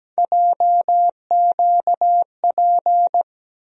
Morse code, heard as a keyed tone, about 17 wpm, 700 Hz. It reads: JQP
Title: USA Call Sign Postfixes - 3 Letters 17wpm